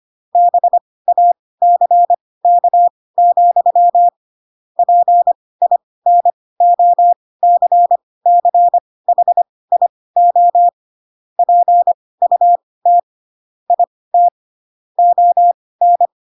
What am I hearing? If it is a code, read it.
Morse code, heard as a keyed tone, about 25 wpm, spaced as for 18 wpm, 700 Hz. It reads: BACK, PINOCCHIO PUT IT ON
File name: Pinocchio ch3 12 25 ik1hge